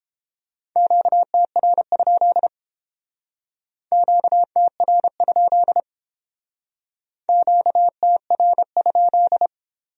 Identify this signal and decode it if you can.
Morse code, heard as a keyed tone, about 30 wpm, 700 Hz. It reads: QTR? QTR? QTR?